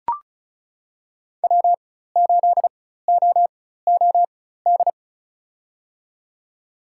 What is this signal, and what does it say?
Morse code, heard as a keyed tone, about 36 wpm, spaced as for 16 wpm, 700 Hz. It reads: W8OOD
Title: USA Call Signs - 5 Characters - 4x Character Spacing - 35wpm